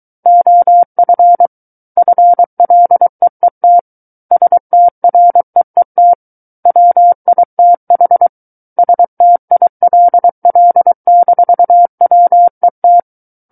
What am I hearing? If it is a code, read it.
Morse code, heard as a keyed tone, about 23 wpm, 700 Hz. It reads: OF FLEET STREET WITH STILL-WET
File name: Book One / 14-Chapter 14 - 23 WPM